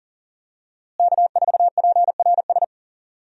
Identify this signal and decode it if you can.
Morse code, heard as a keyed tone, about 40 wpm, 700 Hz. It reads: K4PRS
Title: USA Call Signs - 5 Characters - Repeated 3x - 40wpm